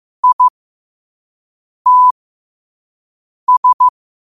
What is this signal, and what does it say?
Morse code, heard as a keyed tone, about 15 wpm, spaced as for 5 wpm, 1000 Hz. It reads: ITS
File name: qotdfiles / 5wpm